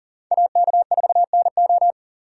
Morse code, heard as a keyed tone, about 50 wpm, 700 Hz. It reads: AK4NO
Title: Speed Racing - USA 2x2 Call Signs 40wpm